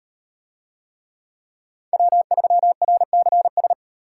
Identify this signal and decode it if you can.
Morse code, heard as a keyed tone, about 38 wpm, 700 Hz. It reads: W3RCS